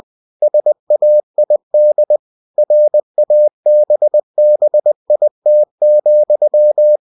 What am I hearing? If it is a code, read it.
Morse code, heard as a keyed tone, about 20 wpm, 600 Hz. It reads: SAID RABBIT,
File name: cw-books / winnie-the-pooh / winnie-20wpm-page-0085